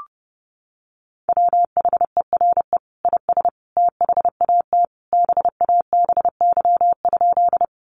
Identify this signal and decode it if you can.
Morse code, heard as a keyed tone, about 30 wpm, 700 Hz. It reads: WHERE IS THAT BABY?
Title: Sentences from Top 1300 Words 30wpm